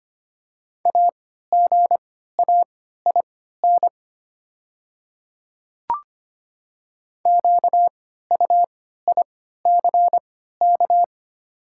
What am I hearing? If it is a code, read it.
Morse code, heard as a keyed tone, about 25 wpm, spaced as for 14 wpm, 700 Hz. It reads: AGAIN QUICK
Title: 3x Character Spacing - Top 500 Words - 25wpm